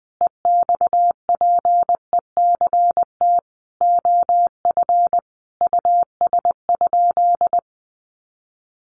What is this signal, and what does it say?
Morse code, heard as a keyed tone, about 20 wpm, 700 Hz. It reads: EXPECT OF US?
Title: Sentences from Top 1400 Words 20wpm